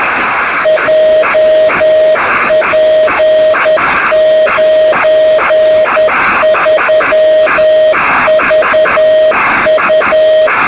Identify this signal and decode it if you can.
Morse code, heard as a keyed tone, about 10 wpm, 630 Hz, with noise. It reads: JP93VU
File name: sm7lcb / 2203z sk2shf-b 23cm